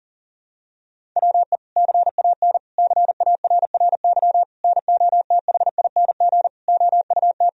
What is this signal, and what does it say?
Morse code, heard as a keyed tone, about 40 wpm, 700 Hz. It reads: WE CAN CARRY NOTHING OUT
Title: Sentences from Top 1000 Words - Review Part 1 - 40wpm